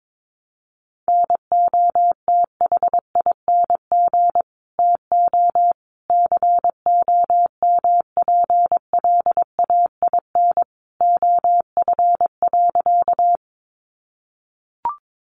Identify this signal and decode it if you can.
Morse code, heard as a keyed tone, about 22 wpm, 700 Hz. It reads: NOTHING TO COMPLAIN OF.
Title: Sentences from Top 1300 Words 22wpm